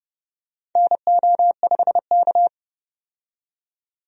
Morse code, heard as a keyed tone, about 30 wpm, 700 Hz. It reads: NO5K
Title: USA Call Signs - 4 Characters - 30wpm